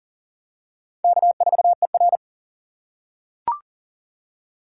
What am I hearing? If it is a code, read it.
Morse code, heard as a keyed tone, about 40 wpm, 700 Hz. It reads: K4ER
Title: USA Call Signs - 4 Characters - 4x Character Spacing - 40wpm